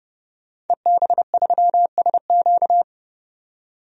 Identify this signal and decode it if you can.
Morse code, heard as a keyed tone, about 29 wpm, 700 Hz. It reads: EB3SQ
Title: DXCC-100 Call Signs - 5 Characters - 30wpm